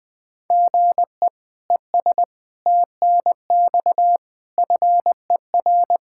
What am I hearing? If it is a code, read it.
Morse code, heard as a keyed tone, about 20 wpm, 700 Hz. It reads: GE ES TNX FER
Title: Practice QSO - Exchange 1 - Beginning - 20wpm